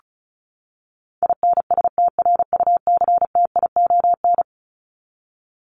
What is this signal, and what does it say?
Morse code, heard as a keyed tone, about 34 wpm, 700 Hz. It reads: INSTRUCTION
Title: Top 1751-2000 Words 35wpm